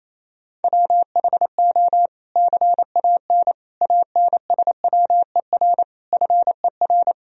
Morse code, heard as a keyed tone, about 28 wpm, 700 Hz. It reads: WHO CAN ANSWER FER
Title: Sentences from Top 300 Words and Abbreviations 28wpm